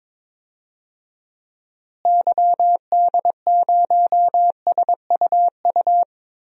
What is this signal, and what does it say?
Morse code, heard as a keyed tone, about 22 wpm, 700 Hz. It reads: YD0SUU
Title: DXCC-100 Call Signs - 6 Characters - Repeated 2x - 22wpm